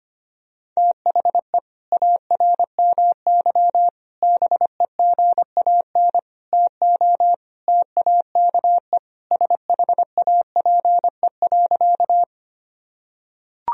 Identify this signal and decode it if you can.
Morse code, heard as a keyed tone, about 25 wpm, 700 Hz. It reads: THE ARMY BEGAN TO TAKE SHAPE.
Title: Sentences from Top 1000 Words - Review Part 3 - 25wpm